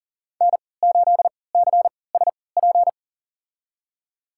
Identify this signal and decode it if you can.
Morse code, heard as a keyed tone, about 40 wpm, spaced as for 23 wpm, 700 Hz. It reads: N8CSP
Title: USA Call Signs - 5 Characters - 3x Character Spacing - 40wpm